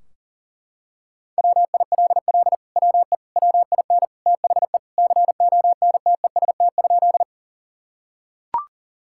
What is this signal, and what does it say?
Morse code, heard as a keyed tone, about 40 wpm, 700 Hz. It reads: WILL WE WIN THE CONTEST?
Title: Sentences from Top 1600 Words 40wpm